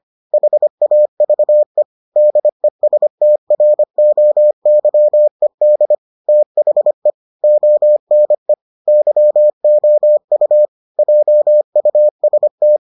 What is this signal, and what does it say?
Morse code, heard as a keyed tone, about 25 wpm, 600 Hz. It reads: HAVE DESTROYED THE ONE YOU JUST